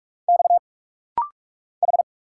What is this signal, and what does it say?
Morse code, heard as a keyed tone, about 46 wpm, 700 Hz. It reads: X H
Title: Single Letters - Rapid-Fire 45wpm